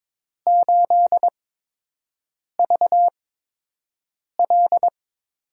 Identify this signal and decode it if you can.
Morse code, heard as a keyed tone, about 22 wpm, spaced as for 6 wpm, 700 Hz. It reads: 8VL
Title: USA Call Signs - 6 Characters - 8x Character Spacing - 22wpm